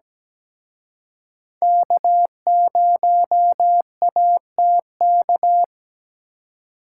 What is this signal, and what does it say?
Morse code, heard as a keyed tone, about 20 wpm, 700 Hz. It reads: K0ATK
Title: Speed Racing - USA 1x3 Call Signs 17wpm